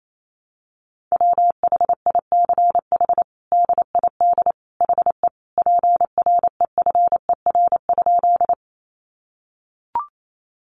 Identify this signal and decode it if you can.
Morse code, heard as a keyed tone, about 28 wpm, 700 Hz. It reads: WHICH DID HE PREFER?
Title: Sentences from Top 1100 Words 28wpm